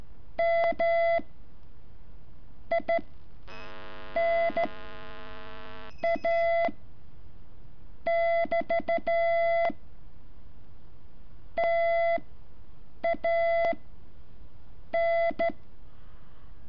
Morse code, heard as a keyed tone, about 13 wpm, spaced as for 7 wpm, 670 Hz. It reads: M I N A <BT> TA N
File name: curso / ejercicio 12